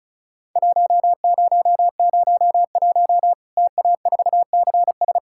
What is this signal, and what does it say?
Morse code, heard as a keyed tone, about 36 wpm, 700 Hz. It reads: 1001 TA4CS
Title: Worked All Europe DX Contest - QTC Exchange - 35wpm